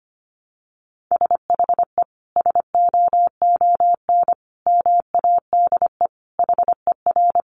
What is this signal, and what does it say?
Morse code, heard as a keyed tone, about 25 wpm, 700 Hz. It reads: SHE SOON MADE HER
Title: Sentences from Top 1100 Words 25wpm